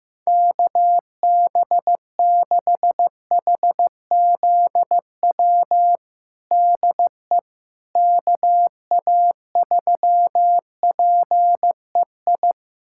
Morse code, heard as a keyed tone, about 15 wpm, 700 Hz. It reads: KB6HZW DE KA3PEI
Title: USA Callsign DE Callsign - 6 Characters - 15wpm